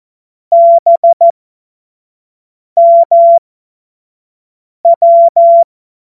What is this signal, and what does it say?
Morse code, heard as a keyed tone, about 14 wpm, spaced as for 5 wpm, 680 Hz. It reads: BMW
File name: Morsekurs / Witze / Witz 07. Dezember 2021, 25 F70